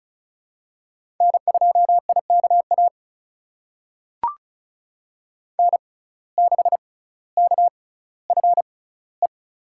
Morse code, heard as a keyed tone, about 36 wpm, spaced as for 18 wpm, 700 Hz. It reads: N2IKA N6KFE
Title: USA Call Signs - 5 Characters - 6x Character Spacing - 35wpm